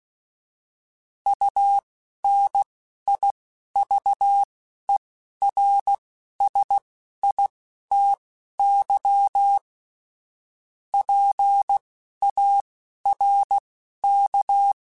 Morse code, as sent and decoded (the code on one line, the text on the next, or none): ..- -. .. ...- . .-. ... .. - -.-- / .--. .- .-. -.-
UNIVERSITY PARK